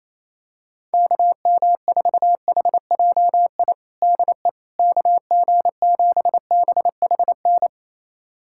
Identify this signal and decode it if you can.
Morse code, heard as a keyed tone, about 28 wpm, 700 Hz. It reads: KM4HJI DE KG7BHN
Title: USA Callsign DE Callsign - 6 Characters - 28wpm